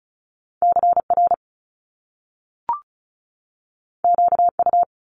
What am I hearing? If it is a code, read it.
Morse code, heard as a keyed tone, about 36 wpm, 700 Hz. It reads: CR QU
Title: Top Two Letter Combinations in Words 35wpm